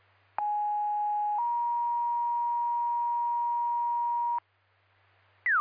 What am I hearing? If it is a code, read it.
Two-tone page, heard as a one-second tone, then a three-second tone, about 830 then 950 Hz. It is heard from 0.4 to 4.4 s.